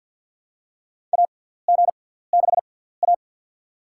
Morse code, heard as a keyed tone, about 47 wpm, spaced as for 17 wpm, 700 Hz. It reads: AG6A
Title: USA Call Signs - 6x Character Spacing - 50wpm